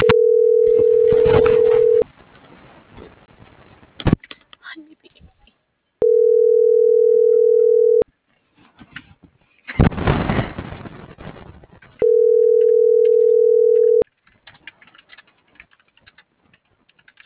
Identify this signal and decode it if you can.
Ringback tone (North America), 440 + 480 Hz.